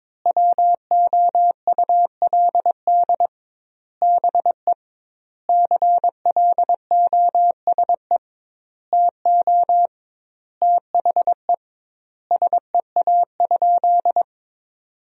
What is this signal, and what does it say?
Morse code, heard as a keyed tone, about 22 wpm, 700 Hz. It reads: WOULD BE CLOSE TO THE SEA?
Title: 2x Word-Spacing - Sentences from Top 300 Words - 22wpm